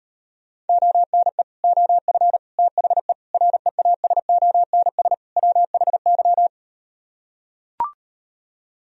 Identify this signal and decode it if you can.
Morse code, heard as a keyed tone, about 38 wpm, 700 Hz. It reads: ONE OF THE REASONS WHY